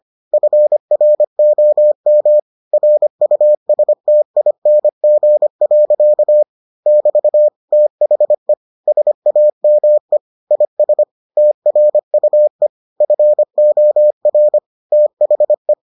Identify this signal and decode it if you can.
Morse code, heard as a keyed tone, about 25 wpm, 600 Hz. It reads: FROM RUSTING. <BT> THE SAME IS TRUE FOR THE